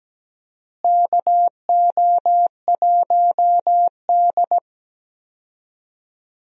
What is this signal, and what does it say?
Morse code, heard as a keyed tone, about 17 wpm, 700 Hz. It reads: KO1D
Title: USA Call Signs - 4 Characters - Repeated 2x - 17wpm